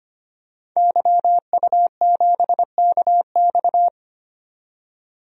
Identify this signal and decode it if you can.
Morse code, heard as a keyed tone, about 25 wpm, 700 Hz. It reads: YU7KX
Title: DXCC-100 Call Signs - 5 Characters - 25wpm